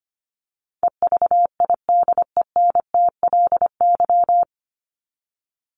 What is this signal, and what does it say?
Morse code, heard as a keyed tone, about 25 wpm, 700 Hz. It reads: EVIDENTLY
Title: LY Suffix - Top 100 Words with Base Words 25wpm